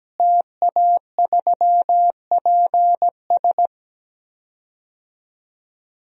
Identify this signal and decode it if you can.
Morse code, heard as a keyed tone, about 17 wpm, 700 Hz. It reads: TA3PS
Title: DXCC-100 Call Signs - 5 Characters - Repeated 3x - 17wpm